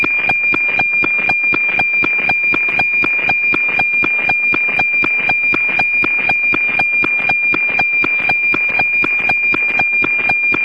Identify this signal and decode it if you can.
Weather fax (radiofax).